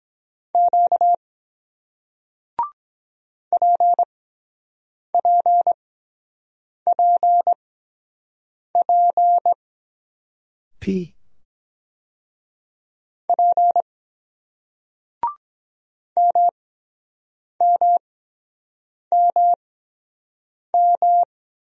Morse code, heard as a keyed tone, about 22 wpm, spaced as for 7 wpm, 700 Hz. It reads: Q PPPP P MMMM